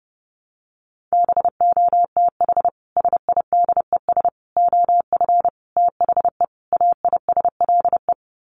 Morse code, heard as a keyed tone, about 30 wpm, 700 Hz. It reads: BOTH SIDES OF THE AISLE
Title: Sets of 5 Words Encore2 30wpm